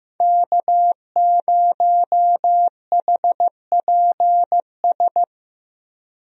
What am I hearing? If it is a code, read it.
Morse code, heard as a keyed tone, about 15 wpm, 700 Hz. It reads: K0HPS
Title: USA Call Signs - 5 Characters - Repeated 3x - 15wpm